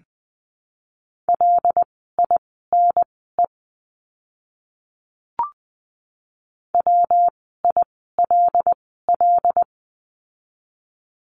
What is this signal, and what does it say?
Morse code, heard as a keyed tone, about 20 wpm, spaced as for 14 wpm, 700 Hz. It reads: LINE WILL
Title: 115 Lesson 19f - Letter w - Words 20wpm